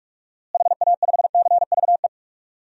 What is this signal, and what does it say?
Morse code, heard as a keyed tone, about 46 wpm, 700 Hz. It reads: HA5CVE
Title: DXCC-100 Call Signs - 6 Characters - Repeated 2x - 45wpm